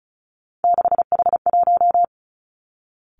Morse code, heard as a keyed tone, about 36 wpm, 700 Hz. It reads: 6H1
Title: Three Challenging Letter-Number Combinations 35wpm